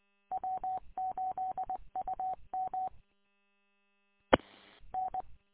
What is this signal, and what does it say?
Morse code, heard as a keyed tone, about 20 wpm, 750 Hz. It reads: W8UM E N
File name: w8um- -combined